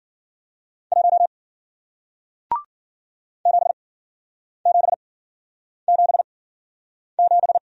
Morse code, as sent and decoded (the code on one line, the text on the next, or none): .---- / --... --... --... --...
1 7777